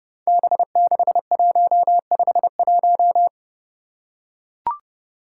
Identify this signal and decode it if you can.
Morse code, heard as a keyed tone, about 28 wpm, 700 Hz. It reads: B6151
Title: Five Challenging Letter-Number Combinations 30wpm